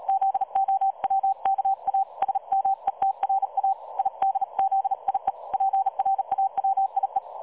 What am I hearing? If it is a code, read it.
Morse code, heard as a keyed tone, about 37 wpm, 780 Hz. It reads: ZOWKA SMETRA IDZIE PRAWIE
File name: dlazaawansowanych / nagraniazpasma / websdr recording 3565.0kHz